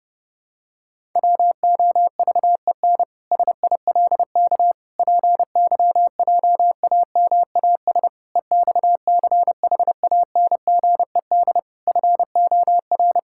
Morse code, heard as a keyed tone, about 30 wpm, 700 Hz. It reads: WOVEN SILK PYJAMAS EXCHANGED FOR
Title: Pangram Challenge v2 - 30wpm